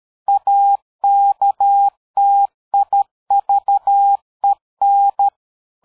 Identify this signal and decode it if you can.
Morse code, heard as a keyed tone, about 13 wpm, 800 Hz. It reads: AKTIVEN
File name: dk5ke / PersWorte 60